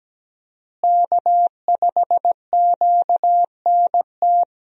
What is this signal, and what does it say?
Morse code, heard as a keyed tone, about 17 wpm, 700 Hz. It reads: K5QNT
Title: USA Call Signs - 5 Characters - Repeated 2x - 17wpm